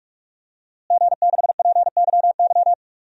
Morse code, heard as a keyed tone, about 46 wpm, 700 Hz. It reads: G6PYY
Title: DXCC-100 Call Signs - 5 Characters - Repeated 3x - 45wpm